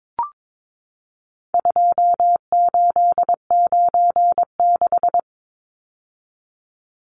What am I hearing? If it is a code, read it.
Morse code, heard as a keyed tone, about 22 wpm, 700 Hz. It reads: 2896
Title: Numbers - 4-Digit 22wpm